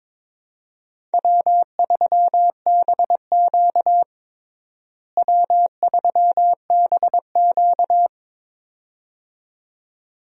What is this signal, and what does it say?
Morse code, heard as a keyed tone, about 22 wpm, 700 Hz. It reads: W3BQ W3BQ
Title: USA Call Signs - 4 Characters - Repeated 2x - 22wpm